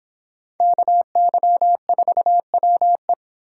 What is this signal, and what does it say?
Morse code, heard as a keyed tone, about 24 wpm, 700 Hz. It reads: KY4WE